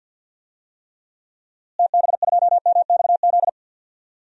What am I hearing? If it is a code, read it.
Morse code, heard as a keyed tone, about 50 wpm, 700 Hz. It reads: TB2MXZ
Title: DXCC-100 Call Signs - 6 Characters - Repeated 2x - 50wpm